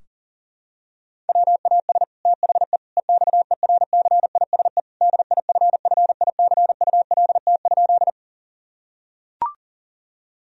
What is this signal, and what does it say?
Morse code, heard as a keyed tone, about 40 wpm, 700 Hz. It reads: WAS THE EXERCISE DIFFICULT?
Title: Sentences from Top 1000 Words - Review Part 4 - 40wpm